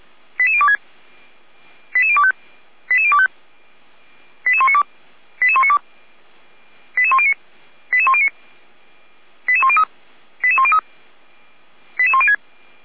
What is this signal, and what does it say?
Five-tone selective call (ZVEI), heard as quick runs of five tones, tones of about 70 ms. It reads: 88026 88025 88025 88192 88192 88109 88109 88103 88103 88106